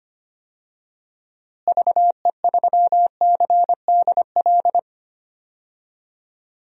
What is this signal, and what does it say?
Morse code, heard as a keyed tone, about 25 wpm, 700 Hz. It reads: VE3CDL
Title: DXCC-100 Call Signs - 6 Characters - Repeated 2x - 25wpm